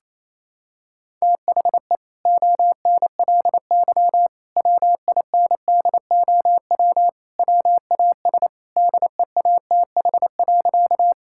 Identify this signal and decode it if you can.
Morse code, heard as a keyed tone, about 28 wpm, 700 Hz. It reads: THE ONLY WINDOW WAS DEATH.
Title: Sentences from Top 1600 Words 28wpm